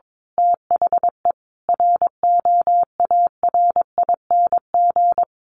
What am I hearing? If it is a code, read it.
Morse code, heard as a keyed tone, about 22 wpm, 700 Hz. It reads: THE ROARING
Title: A Sound Of Thunder 22wpm